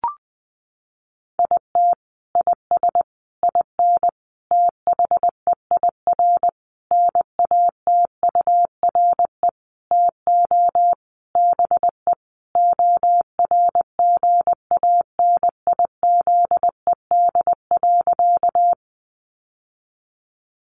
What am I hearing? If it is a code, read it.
Morse code, heard as a keyed tone, about 20 wpm, 700 Hz. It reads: IT IS IN THEIR NATURE TO BE ORGANIZED.
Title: Sentences from Top 1300 Words 20wpm